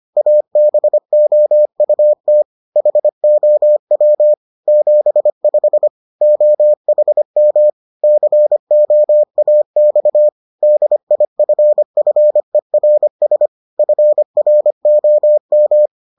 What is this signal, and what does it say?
Morse code, heard as a keyed tone, about 25 wpm, 600 Hz. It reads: ABOUT HOW 75 OHM COAX DIFFERS FROM